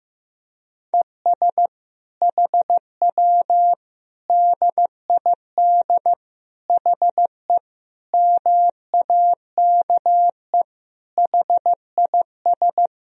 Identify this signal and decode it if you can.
Morse code, heard as a keyed tone, about 15 wpm, 700 Hz. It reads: ES HW DID HE MAKE HIS
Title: Sentences from Top 400 Words and Abbreviations 15wpm